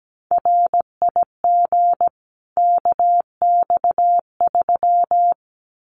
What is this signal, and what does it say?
Morse code, heard as a keyed tone, about 17 wpm, 700 Hz. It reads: RIG KX3
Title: Common QSO Phrases 17wpm